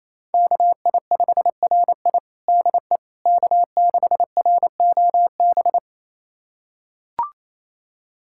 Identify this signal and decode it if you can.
Morse code, heard as a keyed tone, about 28 wpm, 700 Hz. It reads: KI5RI DE K6ROB
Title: USA Callsign DE Callsign - 5 Characters - 28wpm